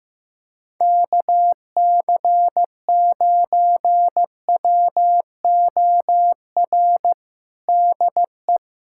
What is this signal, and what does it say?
Morse code, heard as a keyed tone, about 15 wpm, 700 Hz. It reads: KC9WOR DE
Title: USA Callsign DE Callsign 15wpm